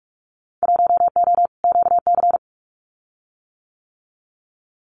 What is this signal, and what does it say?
Morse code, heard as a keyed tone, about 46 wpm, 700 Hz. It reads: 1O QC
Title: Winter Field Day Exchanges 45wpm